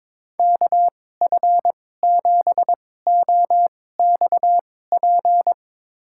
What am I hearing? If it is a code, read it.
Morse code, heard as a keyed tone, about 21 wpm, spaced as for 16 wpm, 700 Hz. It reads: KF7OXP
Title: USA Call Signs - 6 Characters - 2x Character Spacing - 22wpm